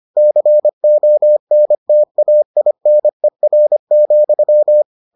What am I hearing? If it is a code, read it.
Morse code, heard as a keyed tone, about 25 wpm, 600 Hz. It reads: CONTAINER,